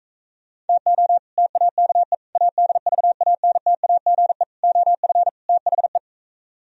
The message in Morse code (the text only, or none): TO TAKE ADVANTAGE OF THE